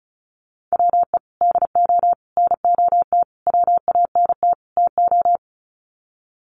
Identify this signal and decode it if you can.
Morse code, heard as a keyed tone, about 34 wpm, 700 Hz. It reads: WE DO NOT WANT TO
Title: Sets of 5 Words 35wpm